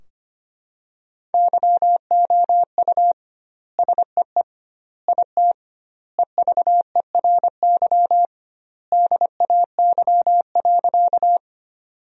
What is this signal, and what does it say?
Morse code, heard as a keyed tone, about 25 wpm, 700 Hz. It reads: YOU SEE IT EVERY DAY.
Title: 2x Word-Spacing - Sentences from Top 200 Words - 25wpm